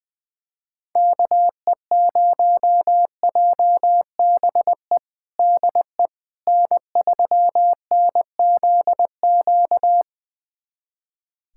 Morse code, heard as a keyed tone, about 20 wpm, 700 Hz. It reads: KE0JBE DE N3NZQ
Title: USA Callsign DE Callsign 20wpm